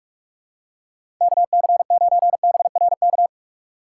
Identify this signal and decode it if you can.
Morse code, heard as a keyed tone, about 44 wpm, 700 Hz. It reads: KC9BRK
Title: USA Call Signs - 6x Character Spacing - 45wpm